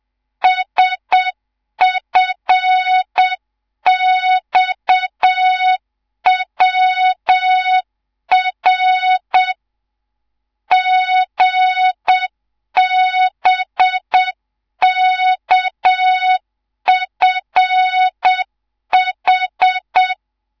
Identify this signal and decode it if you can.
Morse code, heard as a keyed tone, about 7 wpm, 3700 Hz, with noise. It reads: SFXWR GBKFH
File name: Morze-HA2MN / hang / 11 GYAKORLAT